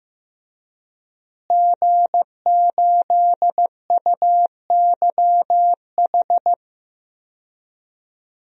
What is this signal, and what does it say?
Morse code, heard as a keyed tone, about 15 wpm, 700 Hz. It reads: G8UYH
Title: DXCC-100 Call Signs - 5 Characters - Repeated 3x - 15wpm